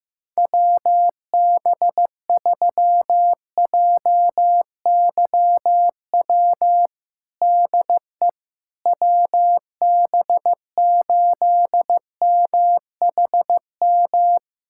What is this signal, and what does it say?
Morse code, heard as a keyed tone, about 15 wpm, 700 Hz. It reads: WB3JYW DE WB8MHM
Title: USA Callsign DE Callsign - 6 Characters - 15wpm